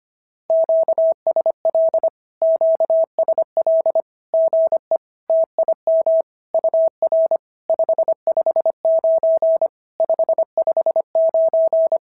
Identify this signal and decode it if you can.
Morse code, heard as a keyed tone, about 24 wpm, 650 Hz. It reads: QSL QSL GE TIM UR 559 559
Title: SOTA QSO Practice 25wpm